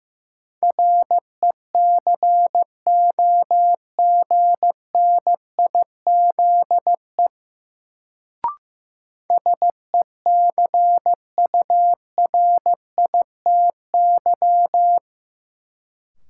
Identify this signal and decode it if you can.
Morse code, heard as a keyed tone, about 15 wpm, 700 Hz. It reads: RECOGNIZE SECURITY
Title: Sentences from Top 1400 Words - New Word Drill Alt - 15wpm